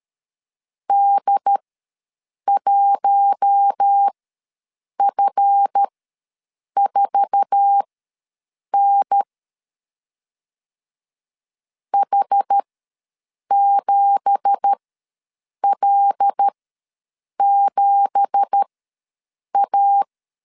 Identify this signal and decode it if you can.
Morse code, heard as a keyed tone, about 13 wpm, spaced as for 7 wpm, 800 Hz. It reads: D1F4N H7L7A
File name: LEZIONI / 9 2867 ws 12wpm 3431